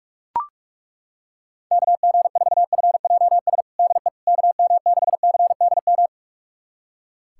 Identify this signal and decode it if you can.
Morse code, heard as a keyed tone, about 41 wpm, 700 Hz. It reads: KG4FJS DE KM6CDM
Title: USA Callsign DE Callsign 45wpm